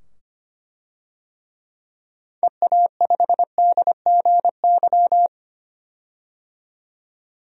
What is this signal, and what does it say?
Morse code, heard as a keyed tone, about 25 wpm, 700 Hz. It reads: EA5DGY
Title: DXCC-100 Call Signs - 6 Characters - 25wpm